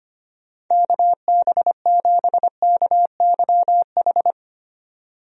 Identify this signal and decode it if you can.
Morse code, heard as a keyed tone, about 25 wpm, 700 Hz. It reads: KB7KYH
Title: USA Call Signs - 6 Characters - 25wpm